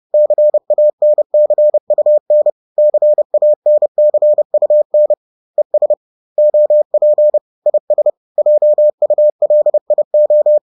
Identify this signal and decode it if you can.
Morse code, heard as a keyed tone, about 30 wpm, 600 Hz. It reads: CANCUN CANCUN ES OP IS JULIO
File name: 30wpm-4